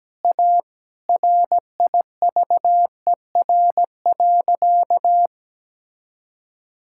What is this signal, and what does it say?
Morse code, heard as a keyed tone, about 17 wpm, 700 Hz. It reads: A RIVER.
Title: Sentences from Top 1000 Words - Review Part 6 - 17wpm